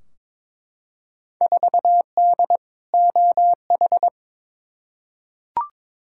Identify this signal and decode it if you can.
Morse code, heard as a keyed tone, about 22 wpm, 700 Hz. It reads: 4D OH
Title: Field Day Exchanges Encore - 22wpm